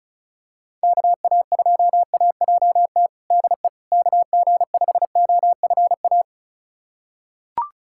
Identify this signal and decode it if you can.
Morse code, heard as a keyed tone, about 34 wpm, 700 Hz. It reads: KA2AJT DE KG5OFA
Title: USA Callsign DE Callsign - 6 Characters - 35wpm